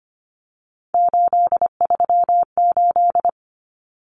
Morse code, heard as a keyed tone, about 25 wpm, 700 Hz. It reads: 838
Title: Numbers - 3-Digit 25wpm